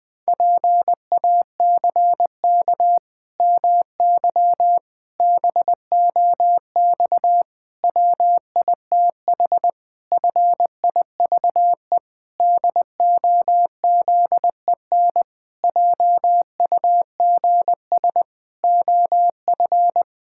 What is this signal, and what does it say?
Morse code, heard as a keyed tone, about 21 wpm, 700 Hz. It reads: PACK MY BOX WITH FIVE DOZEN JUGS OF
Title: Pangram Challenge v2 - 20wpm